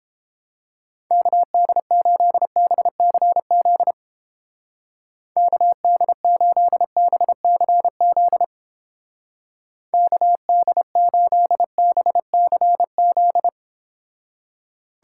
Morse code, heard as a keyed tone, about 29 wpm, 700 Hz. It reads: KD8BCZ KD8BCZ KD8BCZ